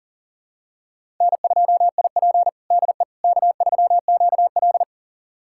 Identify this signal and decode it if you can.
Morse code, heard as a keyed tone, about 40 wpm, 700 Hz. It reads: N2IP DE K3QL
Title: USA Callsign DE Callsign - 4 Characters - 40wpm